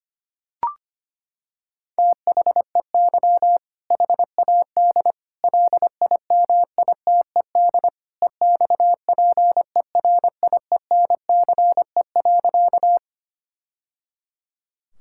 Morse code, heard as a keyed tone, about 25 wpm, 700 Hz. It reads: THEY HAD LIMITED EXPERIENCE.
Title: Sentences from Top 1000 Words - Review Part 6 - 25wpm